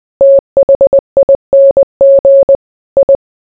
Morse code, heard as a keyed tone, about 20 wpm, 550 Hz. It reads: THING I